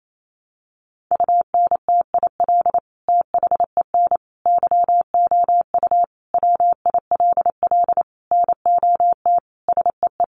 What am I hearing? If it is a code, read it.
Morse code, heard as a keyed tone, about 28 wpm, 700 Hz. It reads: UNTIL THEN YOU WILL NOT SEE
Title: Sentences from Top 600 Words 28wpm